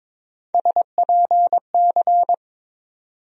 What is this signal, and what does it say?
Morse code, heard as a keyed tone, about 22 wpm, 700 Hz. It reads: SPC
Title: USA Call Sign Postfixes - 3 Letters 22wpm